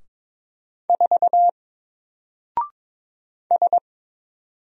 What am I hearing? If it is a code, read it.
Morse code, heard as a keyed tone, about 22 wpm, 700 Hz. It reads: 4 S
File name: Single-Letter-Number-22wpm